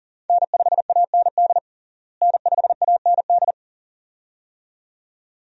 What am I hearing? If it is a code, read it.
Morse code, heard as a keyed tone, about 37 wpm, 700 Hz. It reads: N5AND N5AND
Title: USA Call Signs - 5 Characters - Repeated 2x - 40wpm